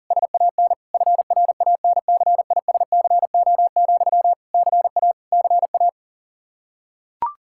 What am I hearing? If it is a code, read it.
Morse code, heard as a keyed tone, about 40 wpm, 700 Hz. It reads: SAN FRANCISCO, CA CA
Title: Practice QSO - Exchange 1 - Beginning - 40wpm